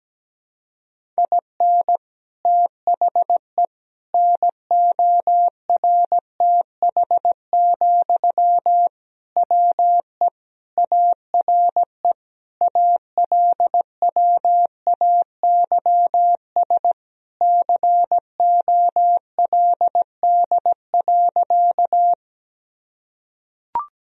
Morse code, heard as a keyed tone, about 17 wpm, 700 Hz. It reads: IN THE NORTH, WE ARE ALWAYS COLD.
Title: Sentences from Top 1000 Words - Review Part 1 - 17wpm